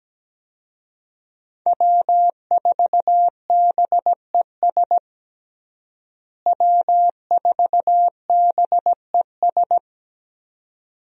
Morse code, heard as a keyed tone, about 17 wpm, 700 Hz. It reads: W4BES W4BES
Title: USA Call Signs - 5 Characters - Repeated 2x - 17wpm